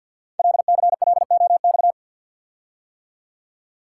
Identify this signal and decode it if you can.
Morse code, heard as a keyed tone, about 52 wpm, 700 Hz. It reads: LCLOX
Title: Code Groups - 5 Letter:Numbers - 50wpm